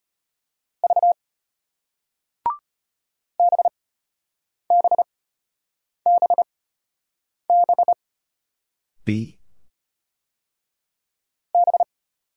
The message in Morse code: V B B B B B

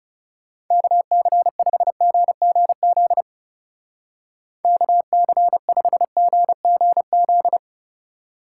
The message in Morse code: KC5GGZ KC5GGZ

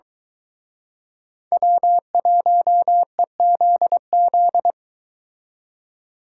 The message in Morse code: W1EZZ

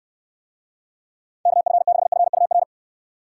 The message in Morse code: KF6FUR

